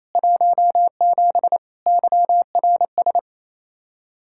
Morse code, heard as a keyed tone, about 34 wpm, 700 Hz. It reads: 17 YRS
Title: Speed-Racing Common QSO Phrases - 28wpm